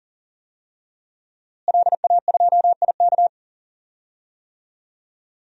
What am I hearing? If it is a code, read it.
Morse code, heard as a keyed tone, about 40 wpm, 700 Hz. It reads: LA2IK
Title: DXCC-100 Call Signs - 5 Characters - Repeated 2x - 40wpm